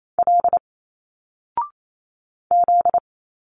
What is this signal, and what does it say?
Morse code, heard as a keyed tone, about 28 wpm, 700 Hz. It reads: L Z
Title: Single Letters 28wpm